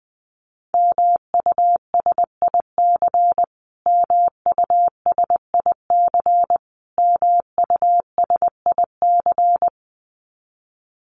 Morse code, heard as a keyed tone, about 20 wpm, 700 Hz. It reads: MUSIC MUSIC MUSIC
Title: Top 500 Words Repeated 20wpm